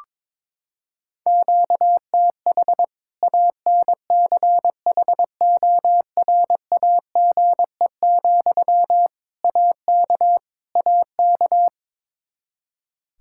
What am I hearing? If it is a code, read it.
Morse code, heard as a keyed tone, about 22 wpm, 700 Hz. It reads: QTH ANCHORAGE, AK AK
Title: Practice QSO - Exchange 1 - Beginning - 22wpm